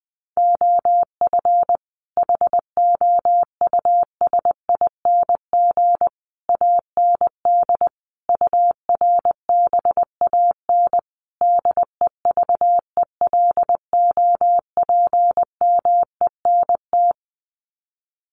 Morse code, heard as a keyed tone, about 20 wpm, 700 Hz. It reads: OF HOUSING AND URBAN DEVELOPMENT
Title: Sets of 5 Words Encore2 20wpm